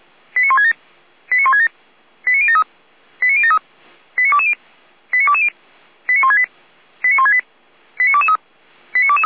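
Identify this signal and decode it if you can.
Five-tone selective call (ZVEI), heard as quick runs of five tones, tones of about 80 ms. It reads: 89167 89167 89073 89073 89220 89220 89168 89168 89223 89223
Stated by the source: ZVEI five-tone call received on SDR, from feuerwehr-orlamuende.org fire brigade server.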